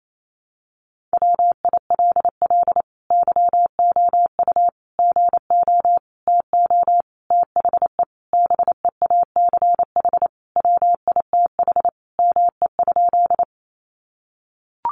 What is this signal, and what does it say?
Morse code, heard as a keyed tone, about 28 wpm, 700 Hz. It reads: WILL YOU GO TO THE BEACH WITH ME?
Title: Sentences from Top 1500 Words 28wpm